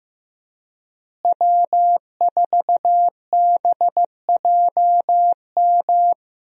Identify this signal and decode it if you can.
Morse code, heard as a keyed tone, about 15 wpm, 700 Hz. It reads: W4BJM
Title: USA Call Signs - 5 Characters - Repeated 3x - 15wpm